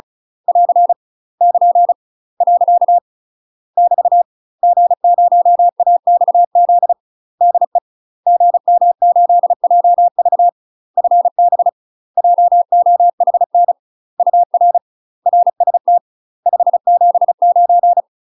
Morse code, heard as a keyed tone, about 35 wpm, 700 Hz. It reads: <AR> <KN> . <BT> G0AXZ DE GM8JV FB JOHN UR RST 579
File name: UK MASTER QSOs TRAINING / 35WPM / UK MASTER QSOs 0021